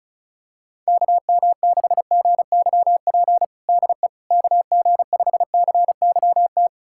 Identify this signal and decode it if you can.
Morse code, heard as a keyed tone, about 34 wpm, 700 Hz. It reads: KM6GYP DE KG5CYT
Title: USA Callsign DE Callsign - 6 Characters - 35wpm